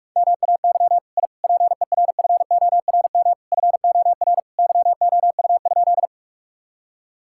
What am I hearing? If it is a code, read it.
Morse code, heard as a keyed tone, about 46 wpm, 700 Hz. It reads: MAY I PERFORM FOR YOU?
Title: Sentences from Top 1000 Words - Review Part 5 - 45wpm